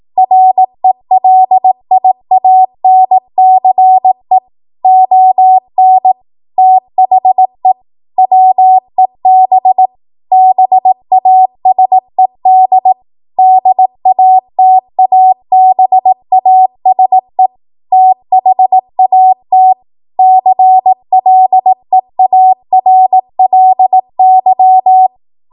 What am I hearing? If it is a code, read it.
Morse code, heard as a keyed tone, about 18 wpm, 750 Hz. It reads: RELIANCE ON THE WEB BASED DATABASE THAT CLEARLY
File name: Morse / 18WPM